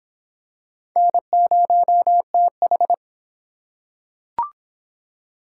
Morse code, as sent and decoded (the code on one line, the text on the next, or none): -. ----- - ....
N0TH